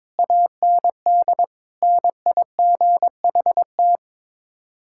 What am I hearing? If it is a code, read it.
Morse code, heard as a keyed tone, about 22 wpm, 700 Hz. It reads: AND NIGHT
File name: practice / 22wpm / Sets of 8 Words - 22wpm